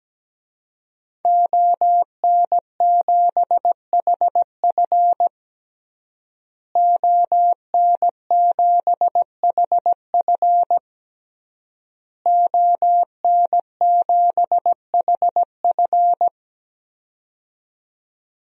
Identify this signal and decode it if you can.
Morse code, heard as a keyed tone, about 17 wpm, 700 Hz. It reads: ON7HF ON7HF ON7HF
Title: DXCC-100 Call Signs - 5 Characters - Repeated 3x - 17wpm